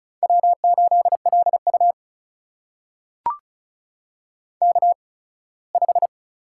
Morse code, heard as a keyed tone, about 36 wpm, spaced as for 21 wpm, 700 Hz. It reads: W8LU K5
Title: USA Call Signs - 4 Characters - 8x Character Spacing - 35wpm